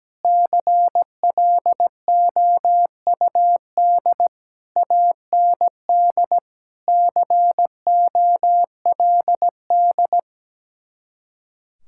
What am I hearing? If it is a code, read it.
Morse code, heard as a keyed tone, about 17 wpm, 700 Hz. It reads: CLOUD AND COLD
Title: Speed-Racing Common QSO Phrases - 17wpm